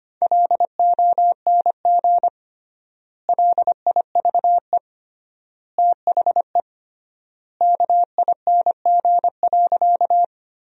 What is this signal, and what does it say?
Morse code, heard as a keyed tone, about 25 wpm, 700 Hz. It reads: LONG LIVE THE KING.
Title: 3x Word-Spacing - Sentences from Top 400 Words - 25wpm